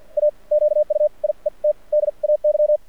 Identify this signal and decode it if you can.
Morse code, heard as a keyed tone, about 52 wpm, spaced as for 39 wpm, 600 Hz. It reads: A QUIET DAY
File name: archived / CRSnet